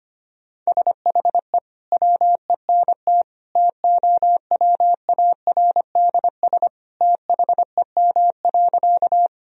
Morse code, heard as a keyed tone, about 25 wpm, 700 Hz. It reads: SHE WENT TOWARDS THEM.
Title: Sentences from Top 1000 Words - Review Part 2 - 25wpm